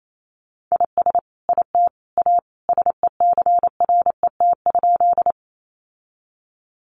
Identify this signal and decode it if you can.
Morse code, heard as a keyed tone, about 28 wpm, 700 Hz. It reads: IS IT A SECRET?
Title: Sentences from Top 1200 Words 28wpm